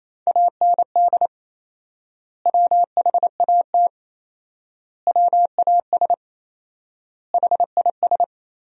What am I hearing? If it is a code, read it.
Morse code, heard as a keyed tone, about 28 wpm, 700 Hz. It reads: AND WHAT WAS HIS
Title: 4x Word-Spacing - Sentences from Top 200 Words - 28wpm